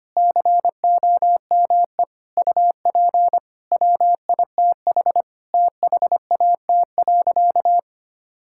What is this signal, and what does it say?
Morse code, heard as a keyed tone, about 25 wpm, 700 Hz. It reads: COME UP WITH THAT.
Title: Sentences from Top 1000 Words - Review Part 5 - 25wpm